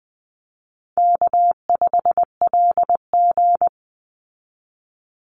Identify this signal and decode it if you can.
Morse code, heard as a keyed tone, about 20 wpm, 700 Hz. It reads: K5LG
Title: Call Signs Repeated 3x - 20wpm